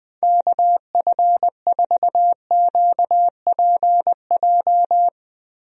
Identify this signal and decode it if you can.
Morse code, heard as a keyed tone, about 20 wpm, 700 Hz. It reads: KF4QPJ
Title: USA Call Signs - 6 Characters - 20wpm